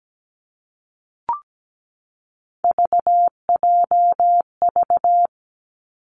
Morse code, heard as a keyed tone, about 17 wpm, 700 Hz. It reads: VJV
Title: Three Challenging Letter-Number Combinations 17wpm